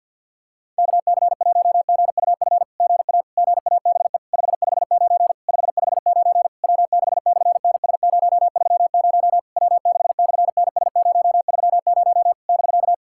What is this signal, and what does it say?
Morse code, heard as a keyed tone, about 50 wpm, 700 Hz. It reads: KC1GUF GA GABE 559 559 W6/NS030 W6/NS030 <BK>
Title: SOTA QSO Practice 50wpm